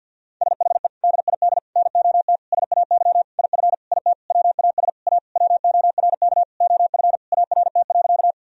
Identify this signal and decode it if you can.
Morse code, heard as a keyed tone, about 50 wpm, 700 Hz. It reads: SHE DID NOT SAY IF IT WAS A WORK OF ART.
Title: Sentences from Top 1000 Words - Review Part 3 - 50wpm